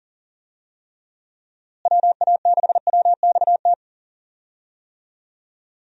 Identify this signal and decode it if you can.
Morse code, heard as a keyed tone, about 40 wpm, 700 Hz. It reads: WA6WXT
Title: USA Call Signs - 6 Characters - Repeated 2x - 40wpm